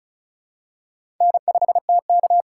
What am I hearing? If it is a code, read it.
Morse code, heard as a keyed tone, about 36 wpm, 700 Hz. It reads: N5TK
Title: USA Call Signs - 4 Characters - 3x Character Spacing - 35wpm